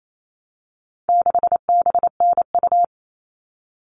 Morse code, heard as a keyed tone, about 28 wpm, 700 Hz. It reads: 6BNU
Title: All Cut Numbers - 4-Digit 28wpm